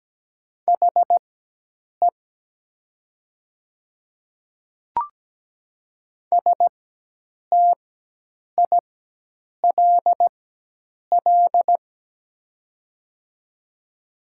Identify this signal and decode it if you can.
Morse code, heard as a keyed tone, about 17 wpm, spaced as for 8 wpm, 700 Hz. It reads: HE STILL
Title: 4x Character Spacing - Top 300 Words - 17wpm